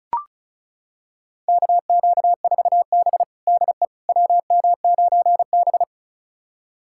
Morse code, heard as a keyed tone, about 36 wpm, 700 Hz. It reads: KQ4B DE WM9B
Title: USA Callsign DE Callsign - 4 Characters - 35wpm